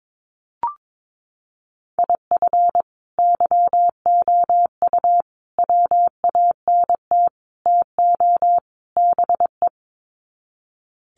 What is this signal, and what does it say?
Morse code, heard as a keyed tone, about 22 wpm, 700 Hz. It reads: IF YOU WANT TO BE
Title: Sets of 5 Words Encore 22wpm